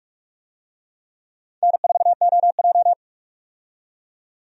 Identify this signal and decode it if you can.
Morse code, heard as a keyed tone, about 46 wpm, 700 Hz. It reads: N4OJ
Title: USA Call Signs - 4 Characters - 45wpm